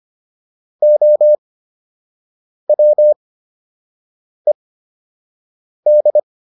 Morse code, heard as a keyed tone, about 25 wpm, spaced as for 6 wpm, 600 Hz. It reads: OWED